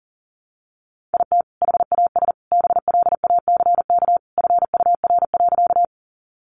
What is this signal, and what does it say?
Morse code, heard as a keyed tone, about 40 wpm, 700 Hz. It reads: IT HAS BLACK FUR.
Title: Easy Phrases 40wpm